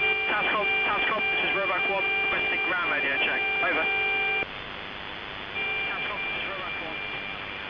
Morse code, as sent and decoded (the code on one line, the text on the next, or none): none